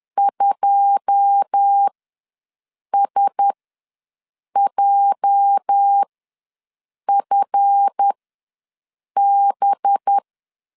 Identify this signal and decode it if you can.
Morse code, heard as a keyed tone, about 10 wpm, spaced as for 6 wpm, 800 Hz. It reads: 2SJFB
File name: LEZIONI / 10 SIGNS / 10A SLASH ws 10wpm 4121